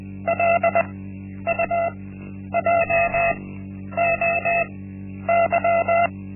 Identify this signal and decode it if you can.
Morse code, heard as a keyed tone, about 20 wpm, spaced as for 10 wpm, 680 Hz. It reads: LUJOY
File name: Linksmakalnis-1